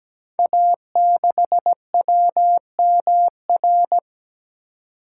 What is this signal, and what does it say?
Morse code, heard as a keyed tone, about 17 wpm, 700 Hz. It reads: A6WMR
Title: Code Groups - 5 Letter:Numbers - 17wpm